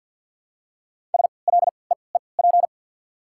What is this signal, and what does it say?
Morse code, heard as a keyed tone, about 54 wpm, spaced as for 29 wpm, 690 Hz. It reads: SLEEP
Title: 3x Character Spacing - Top 500 Words - 50wpm